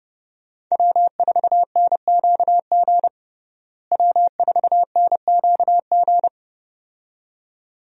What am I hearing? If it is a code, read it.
Morse code, heard as a keyed tone, about 30 wpm, 700 Hz. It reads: W4NQG W4NQG
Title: USA Call Signs - 5 Characters - Repeated 2x - 30wpm